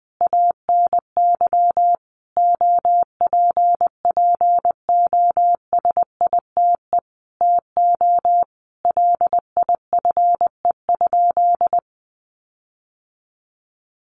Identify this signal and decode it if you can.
Morse code, heard as a keyed tone, about 19 wpm, 700 Hz. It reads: ANY OPPOSITE TO LIFE?
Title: Sentences from Top 1300 Words 20wpm